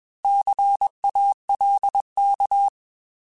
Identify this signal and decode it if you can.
Morse code, heard as a keyed tone, about 21 wpm, 800 Hz. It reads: CALK